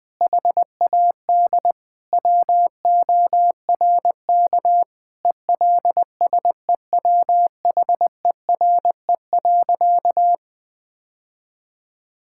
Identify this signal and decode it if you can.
Morse code, heard as a keyed tone, about 20 wpm, 700 Hz. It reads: HAD WORK ELSEWHERE.
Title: Sentences from Top 1600 Words 20wpm